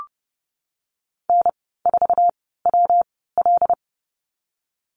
Morse code, heard as a keyed tone, about 30 wpm, 700 Hz. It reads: N 4 W L
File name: Lesson 19g - Letter w - Call Signs 30wpm